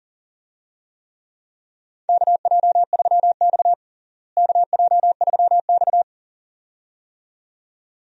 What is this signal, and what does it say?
Morse code, heard as a keyed tone, about 40 wpm, 700 Hz. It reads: KJ3X KJ3X
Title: USA Call Signs - 4 Characters - Repeated 2x - 40wpm